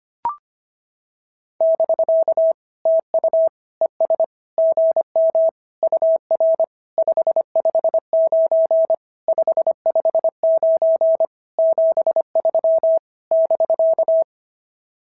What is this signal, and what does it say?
Morse code, heard as a keyed tone, about 24 wpm, 650 Hz. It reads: <BK> TU ES GM UR 559 559 73 <BK>
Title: SOTA QSO Practice 25wpm